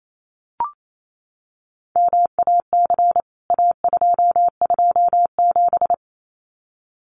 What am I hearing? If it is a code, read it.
Morse code, heard as a keyed tone, about 28 wpm, 700 Hz. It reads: MAC A227
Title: CWT Exchange 28wpm